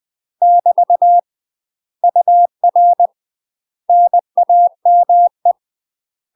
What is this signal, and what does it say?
Morse code, heard as a keyed tone, about 20 wpm, 700 Hz. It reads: <BT> UR NAME
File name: UK MASTER QSOs TRAINING/20WPM